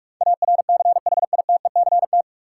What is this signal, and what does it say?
Morse code, heard as a keyed tone, about 46 wpm, 700 Hz. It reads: ARCHITECT
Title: K1USN SST QSO Practice 45wpm